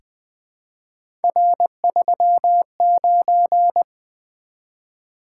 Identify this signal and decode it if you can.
Morse code, heard as a keyed tone, about 20 wpm, 700 Hz. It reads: R39
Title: Three Letter-Number Combinations - 20wpm